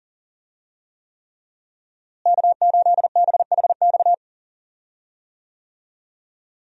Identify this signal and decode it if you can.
Morse code, heard as a keyed tone, about 39 wpm, 700 Hz. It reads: K8BHX
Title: USA Call Signs - 5 Characters - Repeated 3x - 40wpm